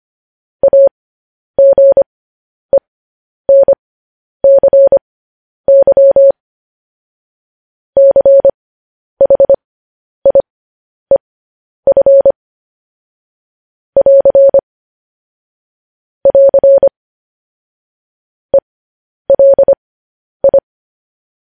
Morse code, heard as a keyed tone, about 25 wpm, spaced as for 10 wpm, 550 Hz. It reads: AGENCY CHIEF <AR> <AR> ELI
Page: news headlines in morse code at 10 wpm